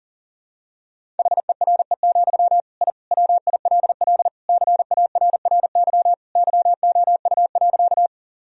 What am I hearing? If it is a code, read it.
Morse code, heard as a keyed tone, about 37 wpm, 700 Hz. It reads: HERE, I WILL CARRY YOU.
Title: Sentences from Top 1000 Words - Review Part 1 - 40wpm